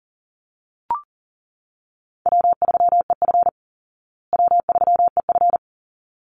Morse code, heard as a keyed tone, about 40 wpm, 700 Hz. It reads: W3EF W3EF
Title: Call Signs Repeated 3x - 40wpm